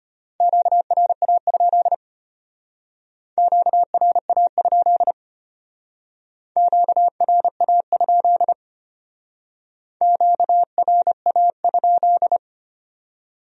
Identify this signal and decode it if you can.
Morse code, heard as a keyed tone, about 30 wpm, 700 Hz. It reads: QRA? QRA? QRA? QRA?
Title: Speed-Racing Q-Codes - 25wpm